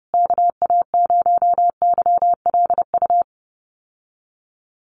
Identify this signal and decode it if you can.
Morse code, heard as a keyed tone, about 30 wpm, 700 Hz. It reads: KA0YLU
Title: USA Call Signs Repeated 2x 30wpm